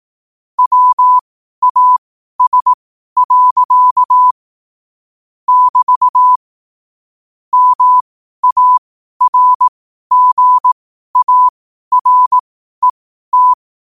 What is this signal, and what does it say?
Morse code, heard as a keyed tone, about 18 wpm, spaced as for 12 wpm, 1000 Hz. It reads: WAS. <BT> MARGARET
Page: 12 wpm morse code quotes